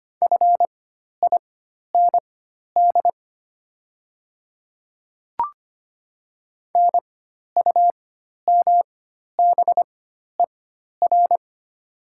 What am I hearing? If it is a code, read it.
Morse code, heard as a keyed tone, about 25 wpm, spaced as for 12 wpm, 690 Hz. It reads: FIND NUMBER
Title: 4x Character Spacing - Top 100 Words - 25wpm